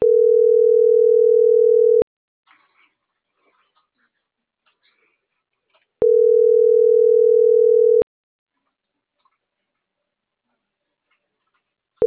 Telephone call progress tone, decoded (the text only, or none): ringback tone